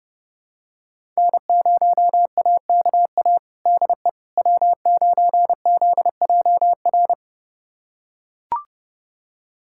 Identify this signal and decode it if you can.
Morse code, heard as a keyed tone, about 31 wpm, 700 Hz. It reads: N0AKA DE W9ZJR
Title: USA Callsign DE Callsign - 5 Characters - 30wpm